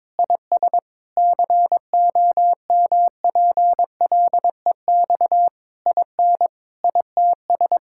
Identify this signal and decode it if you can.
Morse code, heard as a keyed tone, about 22 wpm, 700 Hz. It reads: IS COMPLEX IN ITS
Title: Sentences from Top 1000 Words - Review Part 6 - 22wpm